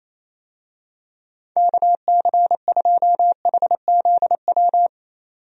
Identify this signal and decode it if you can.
Morse code, heard as a keyed tone, about 28 wpm, 700 Hz. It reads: KC2HZW
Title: USA Call Signs - 6 Characters - 28wpm